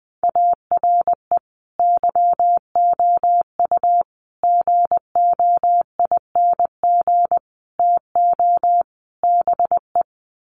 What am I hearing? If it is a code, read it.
Morse code, heard as a keyed tone, about 20 wpm, 700 Hz. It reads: ARE YOU GOING TO BE
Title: Sets of 5 Words Encore 20wpm